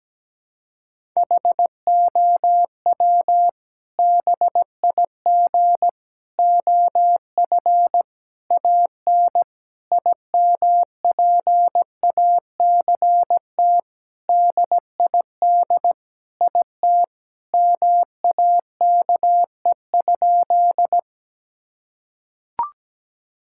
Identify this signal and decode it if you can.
Morse code, heard as a keyed tone, about 17 wpm, 700 Hz. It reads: HOW BIG OF AN IMPACT DID IT MAKE?
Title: Sentences from Top 1000 Words - Review Part 6 - 17wpm